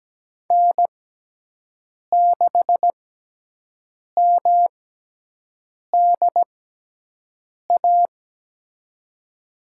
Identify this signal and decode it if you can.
Morse code, heard as a keyed tone, about 17 wpm, spaced as for 6 wpm, 700 Hz. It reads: N6MDA
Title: USA Call Signs - 6x Character Spacing - 17wpm